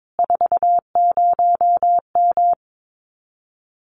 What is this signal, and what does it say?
Morse code, heard as a keyed tone, about 22 wpm, 700 Hz. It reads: 40M
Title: Common QSO Phrases Encore 22wpm